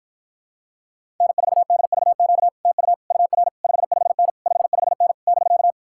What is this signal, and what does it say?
Morse code, heard as a keyed tone, about 54 wpm, 700 Hz. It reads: N4DVQ TU UR 55N 55N <BK>